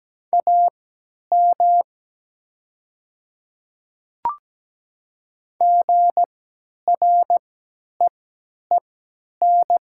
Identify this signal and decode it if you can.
Morse code, heard as a keyed tone, about 17 wpm, spaced as for 10 wpm, 700 Hz. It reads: AM GREEN
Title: 3x Character Spacing - Top 500 Words - 17wpm